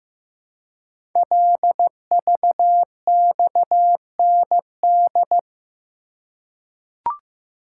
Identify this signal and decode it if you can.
Morse code, heard as a keyed tone, about 15 wpm, 700 Hz. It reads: LVXND
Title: Code Groups - 5 Letters - 15wpm